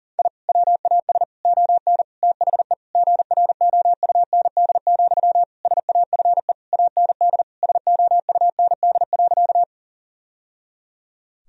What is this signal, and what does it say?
Morse code, heard as a keyed tone, about 40 wpm, 700 Hz. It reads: I WAS ON THE GROUND, SAFE AND SOUND.
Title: Sentences from Top 1000 Words - Review Part 2 - 40wpm